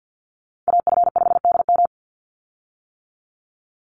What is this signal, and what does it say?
Morse code, heard as a keyed tone, about 50 wpm, 700 Hz. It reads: AF5DM
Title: Call Signs - 50wpm